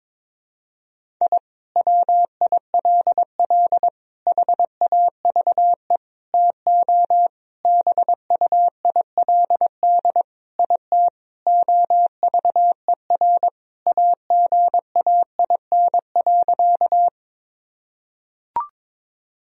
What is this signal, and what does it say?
Morse code, heard as a keyed tone, about 22 wpm, 700 Hz. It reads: I WILL HAVE TO BUILD IT OVER AGAIN.
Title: Sentences from Top 1000 Words - Review Part 1 - 22wpm